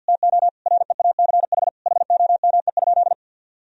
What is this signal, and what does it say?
Morse code, heard as a keyed tone, about 49 wpm, 700 Hz. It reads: TO REACH HOME?
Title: Sentences from Top 1000 Words - Review Part 2 - 50wpm